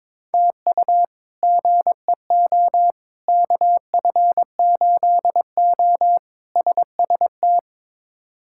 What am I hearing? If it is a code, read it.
Morse code, heard as a keyed tone, about 22 wpm, 700 Hz. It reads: TU GEO KF8O SST
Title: K1USN SST QSO Practice - 22wpm